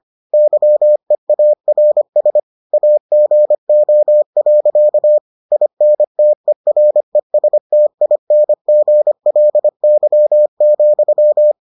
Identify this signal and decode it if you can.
Morse code, heard as a keyed tone, about 25 wpm, 600 Hz. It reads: YEARS AGO. INTERESTINGLY,